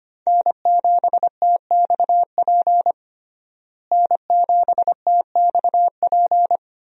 Morse code, heard as a keyed tone, about 25 wpm, 700 Hz. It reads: N7TXP N7TXP
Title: USA Call Signs - 5 Characters - Repeated 2x - 25wpm